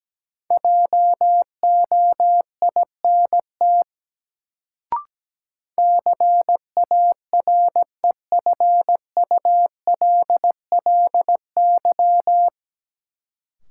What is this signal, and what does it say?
Morse code, heard as a keyed tone, about 17 wpm, 700 Hz. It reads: JOINT CAREFULLY
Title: Sentences from Top 1300 Words - New Word Drill - 17wpm